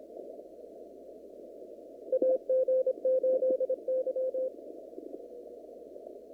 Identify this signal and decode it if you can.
Morse code, heard as a keyed tone, about 27 wpm, 520 Hz. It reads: AG8Y
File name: CW / kt5j wpx cw-op-k5pi